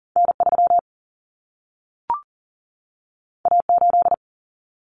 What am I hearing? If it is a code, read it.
Morse code, heard as a keyed tone, about 40 wpm, 700 Hz. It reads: N3 A8
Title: Common Cut Numbers - 2-Digit 40wpm